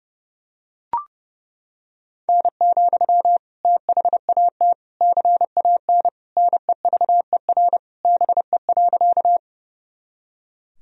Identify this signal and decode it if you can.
Morse code, heard as a keyed tone, about 30 wpm, 700 Hz. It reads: N, THAT CAN NEVER BE.
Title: Sentences from Top 300 Words and Abbreviations 30wpm